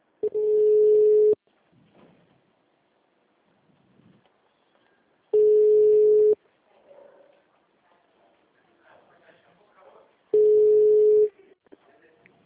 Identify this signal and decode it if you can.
Ringback tone (Europe, 425 Hz), with speech.